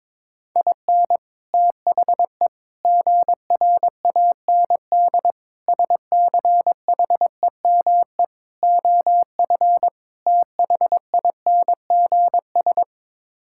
Morse code, heard as a keyed tone, about 22 wpm, 700 Hz. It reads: IN THE GRAND SCHEME OF THINGS
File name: practice / 22wpm / Sets of 6 Words - 22wpm